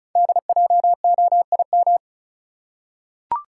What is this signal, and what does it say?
Morse code, heard as a keyed tone, about 36 wpm, 690 Hz. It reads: DJOIM
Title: Code Groups - 5 Letters - 35wpm